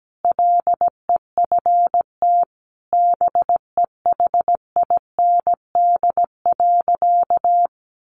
Morse code, heard as a keyed tone, about 17 wpm, 700 Hz. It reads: LEFT BEHIND.
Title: Sentences from Top 1400 Words 17wpm